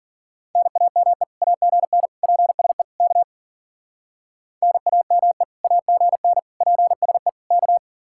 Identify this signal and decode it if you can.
Morse code, heard as a keyed tone, about 51 wpm, 700 Hz. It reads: NAME AGN PSE K NAME AGN PSE K